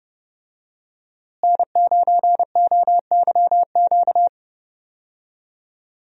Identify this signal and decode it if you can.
Morse code, heard as a keyed tone, about 30 wpm, 700 Hz. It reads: N9OYQ
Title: USA Call Signs - 5 Characters - Repeated 3x - 30wpm